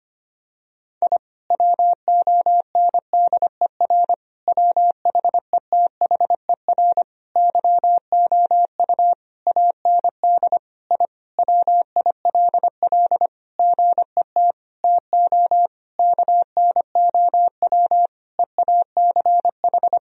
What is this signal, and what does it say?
Morse code, heard as a keyed tone, about 25 wpm, 700 Hz. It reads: I WONDER WHETHER YOU AND I WILL GET TO KNOW EACH
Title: Sentences from Top 1000 Words - Review Part 3 - 25wpm